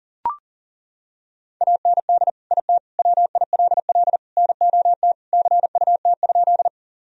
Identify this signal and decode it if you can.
Morse code, heard as a keyed tone, about 40 wpm, 700 Hz. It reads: AND IT WILL NOT CUT?
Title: Sentences from Top 1000 Words - Review Part 1 - 40wpm